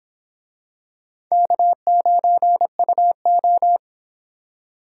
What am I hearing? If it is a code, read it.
Morse code, heard as a keyed tone, about 26 wpm, 700 Hz. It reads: K9UO